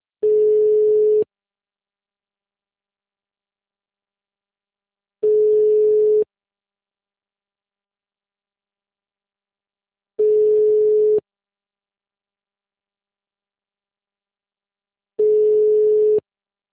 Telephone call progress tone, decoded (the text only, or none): ringback tone